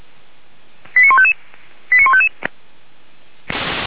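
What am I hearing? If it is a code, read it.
Five-tone selective call (ZVEI), heard as quick runs of five tones, tones of about 70 ms. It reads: 89155 89155
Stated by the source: ZVEI five-tone call 89496 received by SDR, from the Feuerwehr Orlamünde audio archive.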